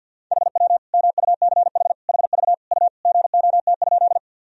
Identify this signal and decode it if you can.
Morse code, heard as a keyed tone, about 52 wpm, 700 Hz. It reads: HW MUCH HV U GOT?
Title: Sentences from Top 300 Words and Abbreviations 50wpm